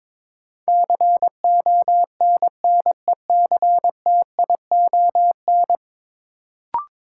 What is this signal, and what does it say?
Morse code, heard as a keyed tone, about 22 wpm, 700 Hz. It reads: CONNECTION
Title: Sentences from Top 1300 Words - New Word Drill - 22wpm